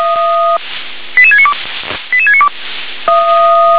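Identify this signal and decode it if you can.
Five-tone selective call (ZVEI), heard as quick runs of five tones, tones of about 70 ms. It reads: 88692 88692
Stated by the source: ZVEI five-tone selective call received via SDR.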